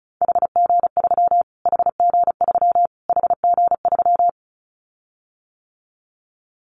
Morse code, heard as a keyed tone, about 34 wpm, 700 Hz. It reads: HG3 HG3 HG3
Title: Callsign Prefixes for Top 25 DXCC - Repeated 3x - 35wpm